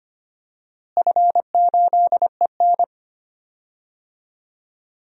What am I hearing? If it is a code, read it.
Morse code, heard as a keyed tone, about 25 wpm, 700 Hz. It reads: F8EN
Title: DXCC-100 Call Signs - 4 Characters - Repeated 2x - 25wpm